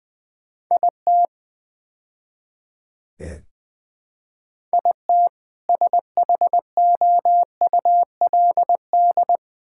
Morse code, heard as a keyed tone, about 21 wpm, 700 Hz. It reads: IT IT SHOULD
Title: Build A Phrase - 5-Word Phrases - 20wpm